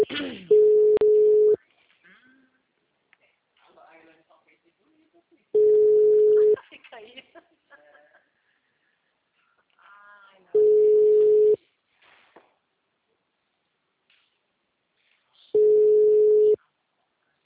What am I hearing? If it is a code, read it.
Ringback tone (Europe, 425 Hz), with speech.